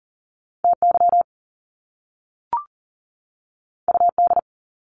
Morse code, heard as a keyed tone, about 39 wpm, 700 Hz. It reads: TY UD
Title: Top Two Letter Combinations in Words 40wpm